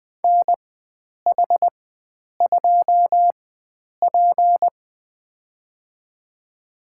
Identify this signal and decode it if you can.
Morse code, heard as a keyed tone, about 20 wpm, spaced as for 9 wpm, 700 Hz. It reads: NH2P
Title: USA Call Signs - 4 Characters - 4x Character Spacing - 20wpm